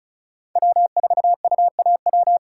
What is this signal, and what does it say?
Morse code, heard as a keyed tone, about 36 wpm, 700 Hz. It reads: W4UAW
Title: USA Call Signs - 4x Character Spacing - 35wpm